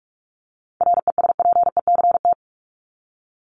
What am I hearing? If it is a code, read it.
Morse code, heard as a keyed tone, about 44 wpm, 700 Hz. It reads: RESPECT
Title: Top 1501-1750 Words 45wpm